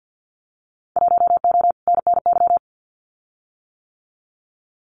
Morse code, heard as a keyed tone, about 50 wpm, 700 Hz. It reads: 1O NNY
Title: Winter Field Day Exchanges 50wpm